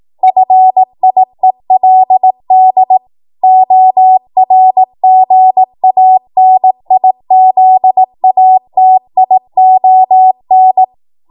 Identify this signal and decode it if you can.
Morse code, heard as a keyed tone, about 18 wpm, 750 Hz. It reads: FIELD ORGANIZATION
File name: Morse / 18WPM